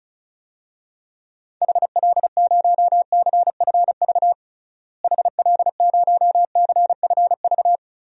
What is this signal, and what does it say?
Morse code, heard as a keyed tone, about 36 wpm, 700 Hz. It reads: HL0CFV HL0CFV
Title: DXCC-100 Call Signs - 6 Characters - Repeated 2x - 35wpm